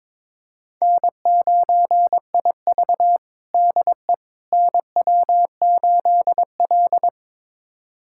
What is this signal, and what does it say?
Morse code, heard as a keyed tone, about 22 wpm, 700 Hz. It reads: N9IV DE NW8L
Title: USA Callsign DE Callsign - 4 Characters - 22wpm